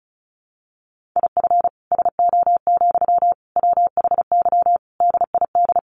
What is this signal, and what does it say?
Morse code, heard as a keyed tone, about 34 wpm, 700 Hz. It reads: IF SO, WHY DID
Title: Sentences from Top 1400 Words 35wpm